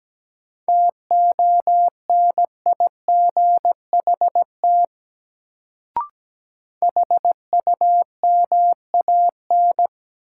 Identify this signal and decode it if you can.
Morse code, heard as a keyed tone, about 17 wpm, 700 Hz. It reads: TONIGHT HUMAN
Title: Sentences from Top 600 Words - New Word Drill - 17wpm